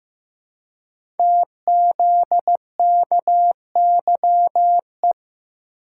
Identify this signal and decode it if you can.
Morse code, heard as a keyed tone, about 15 wpm, 700 Hz. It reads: TZKYE
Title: Code Groups - 5 Letters - 15wpm